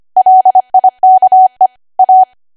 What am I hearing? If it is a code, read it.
Morse code, heard as a keyed tone, about 25 wpm, 750 Hz. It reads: LIKE A
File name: cwa / W1AW-Level3 / 25WPM